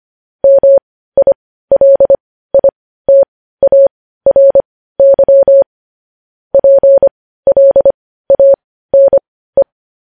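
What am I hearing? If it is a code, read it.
Morse code, heard as a keyed tone, about 25 wpm, spaced as for 15 wpm, 550 Hz. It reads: MILITARY PLANE
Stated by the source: Morse code news headlines at 15.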